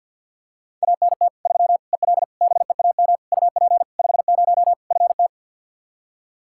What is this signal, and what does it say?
Morse code, heard as a keyed tone, about 47 wpm, 700 Hz. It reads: ANT 3 EL BEAM UP 50 FT
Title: Practice QSO - Exchange 2 - Info - 50wpm